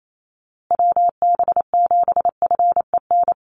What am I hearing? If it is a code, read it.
Morse code, heard as a keyed tone, about 28 wpm, 700 Hz. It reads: WB7FEN
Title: USA Call Signs Repeated 3x 28wpm